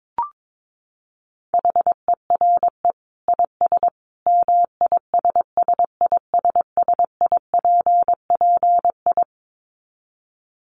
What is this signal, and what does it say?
Morse code, heard as a keyed tone, about 22 wpm, 700 Hz. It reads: HERE IS MISSISSIPPI
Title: Dit Challenge 22wpm